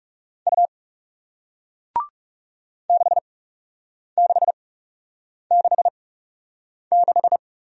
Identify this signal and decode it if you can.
Morse code, heard as a keyed tone, about 39 wpm, spaced as for 8 wpm, 700 Hz. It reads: U 6666